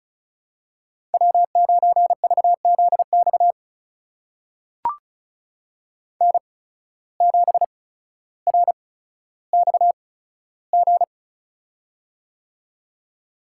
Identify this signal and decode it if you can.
Morse code, heard as a keyed tone, about 34 wpm, spaced as for 15 wpm, 700 Hz. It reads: W9VZX N7RXG
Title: USA Call Signs - 5 Characters - 8x Character Spacing - 35wpm